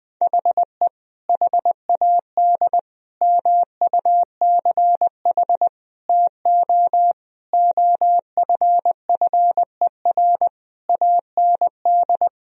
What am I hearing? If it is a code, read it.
Morse code, heard as a keyed tone, about 20 wpm, 700 Hz. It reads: HE HAD MUCH TO OFFER AND
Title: Sentences from Top 1000 Words - Review Part 4 - 20wpm